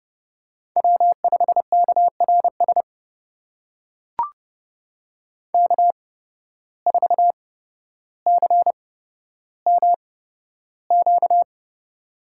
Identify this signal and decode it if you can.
Morse code, heard as a keyed tone, about 29 wpm, spaced as for 13 wpm, 700 Hz. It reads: W5KRS K4CMQ
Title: USA Call Signs - 5 Characters - 8x Character Spacing - 30wpm